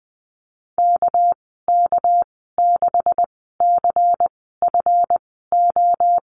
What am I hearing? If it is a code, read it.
Morse code, heard as a keyed tone, about 19 wpm, spaced as for 14 wpm, 700 Hz. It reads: KK6CFO
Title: 219 Lesson 35g - Number 0 - Call Signs 20wpm